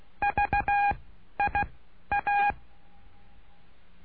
Morse code, heard as a keyed tone, about 16 wpm, spaced as for 11 wpm, 810 Hz. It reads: VIA
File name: cursos / CURSO DE TELEGRAFIA / Telegrafia - Trilha 09